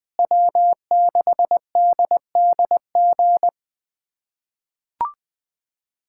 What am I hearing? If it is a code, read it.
Morse code, heard as a keyed tone, about 21 wpm, 700 Hz. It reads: W6DDG
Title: USA Callsign DE Callsign 20wpm